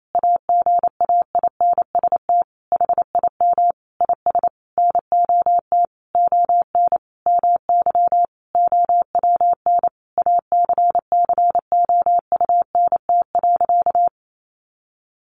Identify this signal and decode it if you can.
Morse code, heard as a keyed tone, about 28 wpm, 700 Hz. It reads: AGAINST HIM IS NOT ON MY OWN ACCOUNT.
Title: Sentences from Top 1400 Words 28wpm